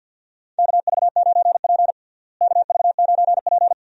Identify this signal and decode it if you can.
Morse code, heard as a keyed tone, about 53 wpm, 700 Hz. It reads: KV9P KV9P
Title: USA Call Signs - 4 Characters - Repeated 3x - 50wpm